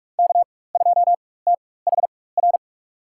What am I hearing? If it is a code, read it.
Morse code, heard as a keyed tone, about 44 wpm, spaced as for 21 wpm, 700 Hz. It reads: K2THR
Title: USA Call Signs - 5 Characters - 4x Character Spacing - 45wpm